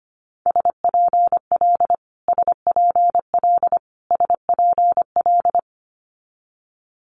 Morse code, heard as a keyed tone, about 25 wpm, 700 Hz. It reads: SPL SPL SPL
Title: Multi-letter Phonemes Repeated 3x 25wpm